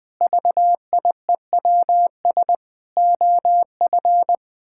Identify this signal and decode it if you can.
Morse code, heard as a keyed tone, about 21 wpm, 700 Hz. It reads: VIEWS OF
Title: Sentences with 8-Word Phrases - Spoken Beginning - 20wpm